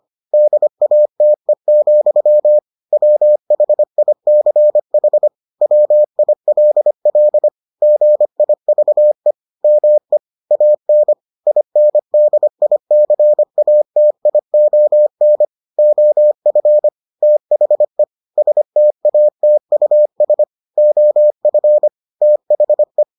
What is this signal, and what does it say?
Morse code, heard as a keyed tone, about 25 wpm, 600 Hz. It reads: DATE, WHICH WILL GIVE ME AN INDICATION OF THE STATUS OF THE